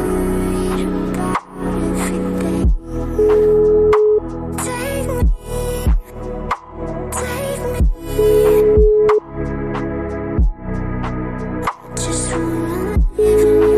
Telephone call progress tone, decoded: ringback tone

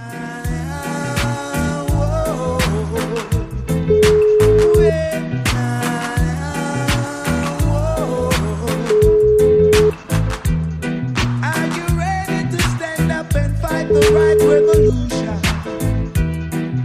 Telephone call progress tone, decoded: ringback tone

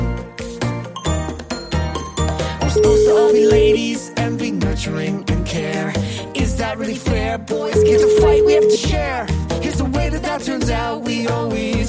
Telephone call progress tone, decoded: ringback tone

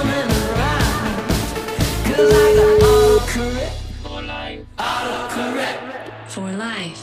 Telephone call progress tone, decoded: ringback tone

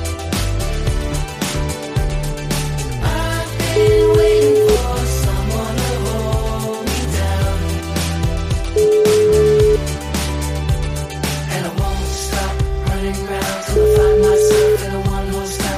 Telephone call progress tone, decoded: ringback tone

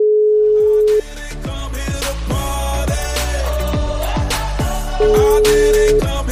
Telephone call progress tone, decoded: ringback tone

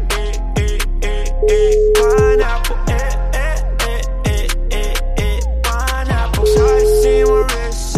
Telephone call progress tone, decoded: ringback tone

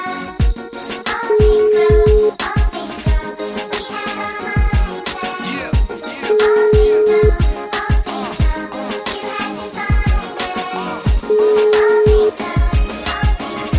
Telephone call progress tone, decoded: ringback tone

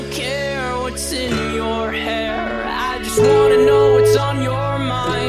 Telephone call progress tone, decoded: ringback tone

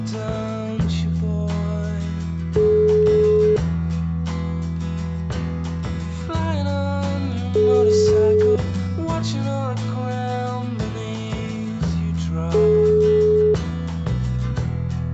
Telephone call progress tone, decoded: ringback tone

